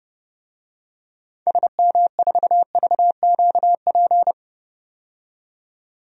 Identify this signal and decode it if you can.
Morse code, heard as a keyed tone, about 32 wpm, 700 Hz. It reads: SM4VQP
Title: DXCC-100 Call Signs - 6 Characters - 30wpm